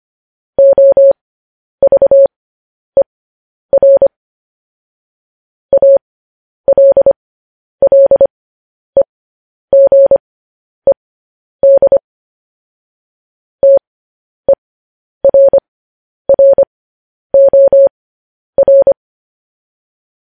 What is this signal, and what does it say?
Morse code, heard as a keyed tone, about 25 wpm, spaced as for 10 wpm, 550 Hz. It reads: OVER ALLEGED TERROR
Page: news-headlines-in-morse-code-at-10-wpm